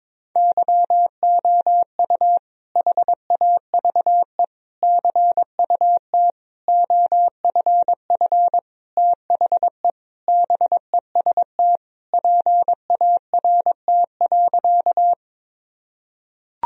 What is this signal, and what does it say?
Morse code, heard as a keyed tone, about 21 wpm, 700 Hz. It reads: YOU HAVE CUT OFF THE BEST PART.
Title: Sentences from Top 1000 Words - Review Part 1 - 22wpm